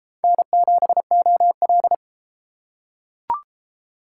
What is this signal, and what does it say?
Morse code, heard as a keyed tone, about 31 wpm, 700 Hz. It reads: N7OL